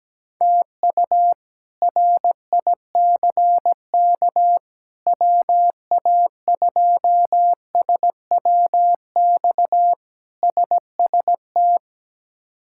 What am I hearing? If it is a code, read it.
Morse code, heard as a keyed tone, about 17 wpm, 700 Hz. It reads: TU RICK WA2SWX SST
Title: K1USN SST QSO Practice - 17wpm